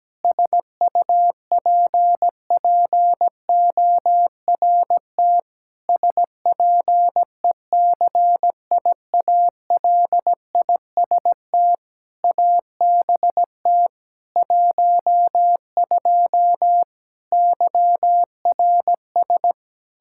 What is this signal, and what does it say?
Morse code, heard as a keyed tone, about 17 wpm, 700 Hz. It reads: SUPPORT SPECIALIST ABT 12 YRS
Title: Practice QSO - Exchange 3 - More Info - 17wpm